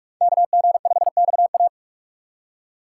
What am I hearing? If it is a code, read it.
Morse code, heard as a keyed tone, about 44 wpm, 700 Hz. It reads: KG5XA